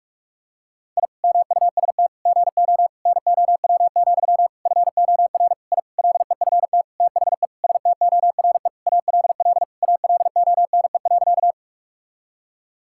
Morse code, heard as a keyed tone, about 43 wpm, 700 Hz. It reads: I MUST GO NOW, FOR I LEFT THE STORE ALL ALONE.
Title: Sentences from Top 1000 Words - Review Part 5 - 45wpm